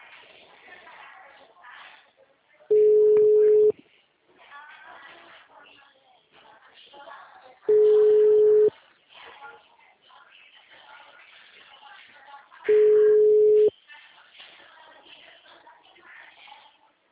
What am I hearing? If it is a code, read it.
Ringback tone (Europe, 425 Hz), with speech.